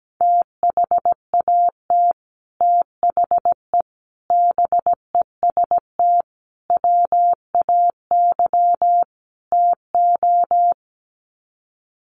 Morse code, heard as a keyed tone, about 17 wpm, 700 Hz. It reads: THAT THE BEST WAY TO
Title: Sets of 5 Words Encore2 17wpm